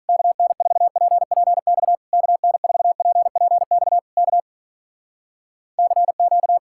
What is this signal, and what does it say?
Morse code, heard as a keyed tone, about 46 wpm, 700 Hz. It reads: KN4PPX KN4PPX K CQ